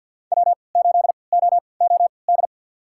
Morse code, heard as a keyed tone, about 51 wpm, spaced as for 28 wpm, 700 Hz. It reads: W8OOD
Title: USA Call Signs - 5 Characters - 3x Character Spacing - 50wpm